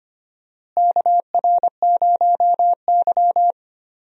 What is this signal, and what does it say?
Morse code, heard as a keyed tone, about 25 wpm, 700 Hz. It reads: KR0Y
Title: USA Call Signs - 4 Characters - Repeated 2x - 25wpm